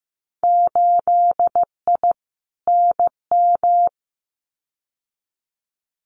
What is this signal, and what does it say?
Morse code, heard as a keyed tone, about 15 wpm, 700 Hz. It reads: 8I NM
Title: Winter Field Day Exchanges Encore 15wpm